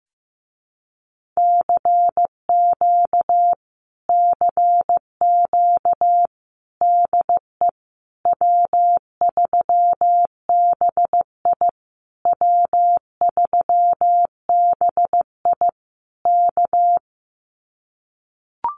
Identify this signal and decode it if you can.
Morse code, heard as a keyed tone, about 15 wpm, 700 Hz. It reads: CQ CQ DE W3BI W3BI K
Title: Common QSO Phrases Encore 15wpm